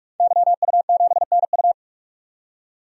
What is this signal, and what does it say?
Morse code, heard as a keyed tone, about 47 wpm, 700 Hz. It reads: YU7NU
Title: DXCC-100 Call Signs - 5 Characters - 45wpm